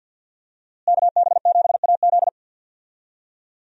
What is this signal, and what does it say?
Morse code, heard as a keyed tone, about 50 wpm, 700 Hz. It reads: KB7AZ
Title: USA Call Signs - 5 Characters - Repeated 3x - 50wpm